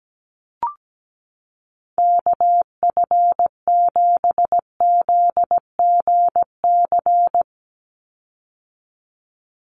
Morse code, heard as a keyed tone, about 17 wpm, 700 Hz. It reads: KF7ZGC
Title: USA Call Signs 17wpm